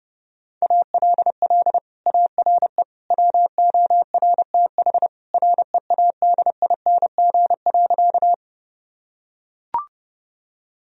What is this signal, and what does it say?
Morse code, heard as a keyed tone, about 32 wpm, 700 Hz. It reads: ALL ARE WORTH READING.
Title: Sentences from Top 1000 Words - Review Part 3 - 30wpm